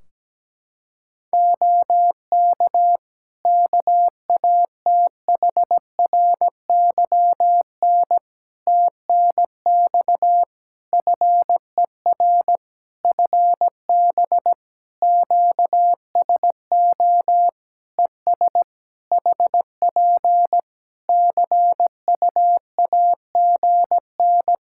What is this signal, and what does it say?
Morse code, heard as a keyed tone, about 17 wpm, 700 Hz. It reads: OK KATHRYN TNX FER FB QSO ES HP CUAGN
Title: Practice QSO - Exchange 4 - Ending - 17wpm